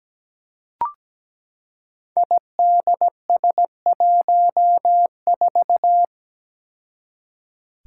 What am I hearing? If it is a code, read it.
Morse code, heard as a keyed tone, about 17 wpm, 700 Hz. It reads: IDS14
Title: Five Challenging Letter-Number Combinations 17wpm